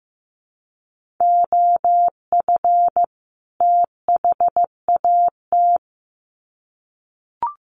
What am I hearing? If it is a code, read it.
Morse code, heard as a keyed tone, about 15 wpm, 700 Hz. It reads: OF THAT
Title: Sets of 2 Words 15wpm